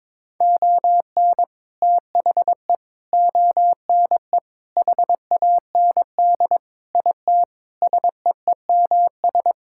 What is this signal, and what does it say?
Morse code, heard as a keyed tone, about 22 wpm, 700 Hz. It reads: ON THE ONE HAND IT SEEMS
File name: practice / 22wpm / Sentences with 4-Word Phrases - 22wpm